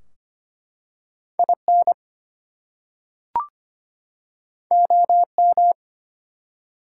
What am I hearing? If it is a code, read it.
Morse code, heard as a keyed tone, about 25 wpm, 700 Hz. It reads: IN OM
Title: Top 180 QSO Elements 25wpm